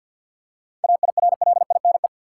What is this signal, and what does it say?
Morse code, heard as a keyed tone, about 51 wpm, 700 Hz. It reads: AIRLINE
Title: Seven Letter Words - 50wpm